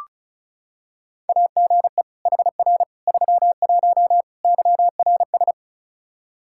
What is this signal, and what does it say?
Morse code, heard as a keyed tone, about 36 wpm, 700 Hz. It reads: AGE HR 31 YRS
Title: Practice QSO - Exchange 3 - More Info - 35wpm